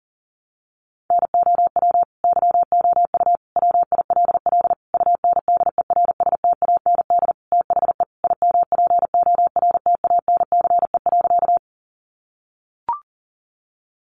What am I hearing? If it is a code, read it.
Morse code, heard as a keyed tone, about 40 wpm, 700 Hz. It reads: NOW YOU WILL UNDERSTAND THE IMPORTANCE.
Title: Sentences from Top 1200 Words 40wpm